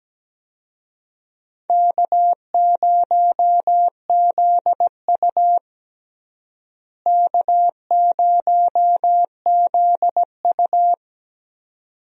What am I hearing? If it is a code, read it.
Morse code, heard as a keyed tone, about 17 wpm, 700 Hz. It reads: K0ZU K0ZU
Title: USA Call Signs - 4 Characters - Repeated 2x - 17wpm